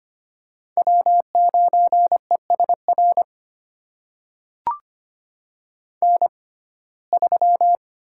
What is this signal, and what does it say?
Morse code, heard as a keyed tone, about 25 wpm, spaced as for 18 wpm, 700 Hz. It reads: W9ESR N3
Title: USA Call Signs - 5 Characters - 6x Character Spacing - 25wpm